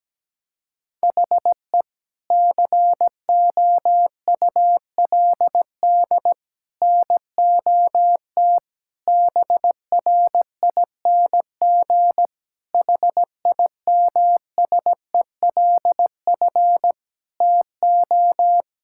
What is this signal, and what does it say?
Morse code, heard as a keyed tone, about 17 wpm, 700 Hz. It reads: HE COULD NOT BRING HIMSELF TO
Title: Sentences from Top 1600 Words 17wpm